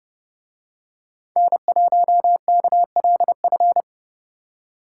Morse code, heard as a keyed tone, about 29 wpm, 700 Hz. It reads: N1KLF